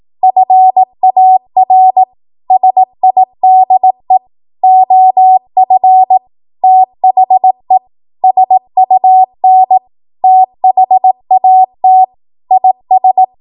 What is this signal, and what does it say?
Morse code, heard as a keyed tone, about 18 wpm, 750 Hz. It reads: FAR SIDE OF THE SUN THAT IS
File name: Morse / 18WPM